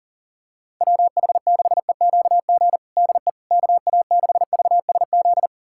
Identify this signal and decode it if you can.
Morse code, heard as a keyed tone, about 40 wpm, 700 Hz. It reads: WH6EQG DE KA6VSZ
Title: USA Callsign DE Callsign - 6 Characters - 40wpm